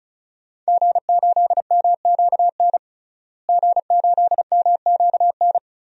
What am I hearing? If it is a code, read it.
Morse code, heard as a keyed tone, about 36 wpm, 700 Hz. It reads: G8MQN G8MQN
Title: DXCC-100 Call Signs - 5 Characters - Repeated 3x - 35wpm